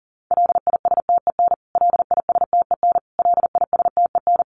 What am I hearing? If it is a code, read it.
Morse code, heard as a keyed tone, about 40 wpm, 700 Hz. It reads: LISTEN LISTEN LISTEN
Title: Top 500 Words Repeated 40wpm